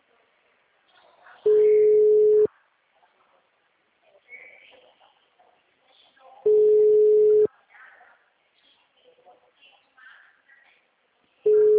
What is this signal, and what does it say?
Ringback tone (Europe, 425 Hz), with speech.